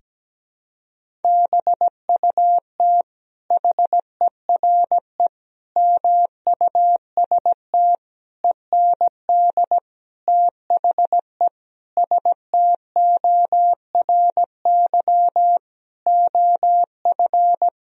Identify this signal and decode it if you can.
Morse code, heard as a keyed tone, about 17 wpm, 700 Hz. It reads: BUT HERE MUST END THE STORY OF
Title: Sentences from Top 1000 Words - Review Part 1 - 17wpm